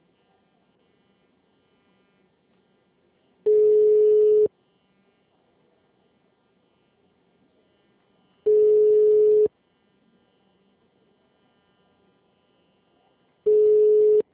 Ringback tone (Europe, 425 Hz).